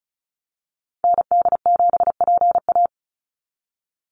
Morse code, heard as a keyed tone, about 36 wpm, 700 Hz. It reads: ND7PA
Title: NA SOTA Activator Callsigns 35wpm